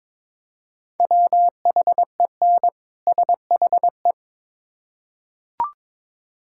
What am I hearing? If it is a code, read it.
Morse code, heard as a keyed tone, about 22 wpm, 700 Hz. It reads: WHEN SHE
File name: Build A Phrase - 2-Word Phrases - 22wpm